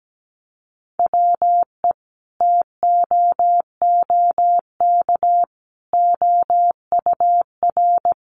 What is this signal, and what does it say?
Morse code, heard as a keyed tone, about 17 wpm, 700 Hz. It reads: WE TOOK OUR
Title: Sentences from Top 400 Words 17wpm